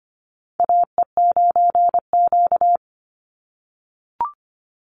Morse code, heard as a keyed tone, about 25 wpm, 700 Hz. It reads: AE9Q
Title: NA SOTA Chaser Callsigns 25wpm